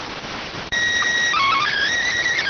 SSTV (slow-scan television), PD 290.